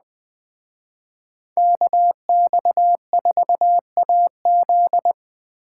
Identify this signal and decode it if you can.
Morse code, heard as a keyed tone, about 20 wpm, 700 Hz. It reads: KX4AZ
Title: Speed Racing - USA 2x2 Call Signs 17wpm